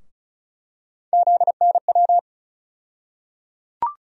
Morse code, heard as a keyed tone, about 34 wpm, 700 Hz. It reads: ZNW
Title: USA Call Sign Postfixes - 3 Letters 35wpm